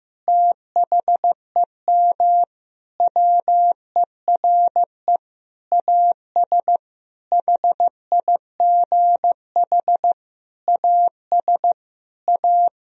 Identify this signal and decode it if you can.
Morse code, heard as a keyed tone, about 16 wpm, 700 Hz. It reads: THEM WERE AS HIGH AS A
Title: Sentences from Top 1000 Words - Review Part 1 - 15wpm